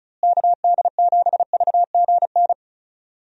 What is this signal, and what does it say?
Morse code, heard as a keyed tone, about 36 wpm, 700 Hz. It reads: KD7VGN
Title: USA Call Signs - 6 Characters - Repeated 2x - 35wpm